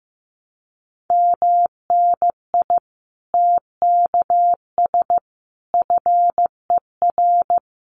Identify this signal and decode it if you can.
Morse code, heard as a keyed tone, about 15 wpm, 700 Hz. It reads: MNI TKS FER
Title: Common QSO Phrases 15wpm